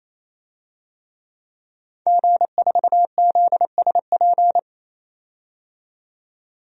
Morse code, heard as a keyed tone, about 28 wpm, 700 Hz. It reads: G4ZSP
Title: DXCC-100 Call Signs - 5 Characters - 28wpm